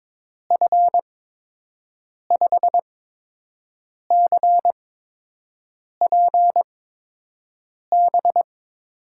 Morse code, heard as a keyed tone, about 22 wpm, spaced as for 6 wpm, 700 Hz. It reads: F5CPB
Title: USA Call Signs - 8x Character Spacing - 22wpm